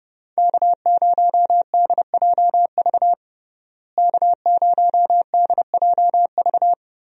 Morse code, heard as a keyed tone, about 30 wpm, 700 Hz. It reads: K0DJV K0DJV
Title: USA Call Signs - 5 Characters - Repeated 2x - 30wpm